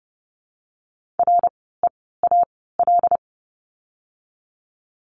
Morse code, heard as a keyed tone, about 30 wpm, spaced as for 17 wpm, 700 Hz. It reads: REAL
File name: course / 30:10wpm / 108 Lesson 18f - Letter m - Words 30wpm